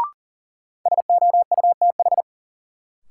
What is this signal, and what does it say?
Morse code, heard as a keyed tone, about 40 wpm, 700 Hz. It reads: SOUTH
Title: Sentences from Top 400 Words - New Word Drill - 40wpm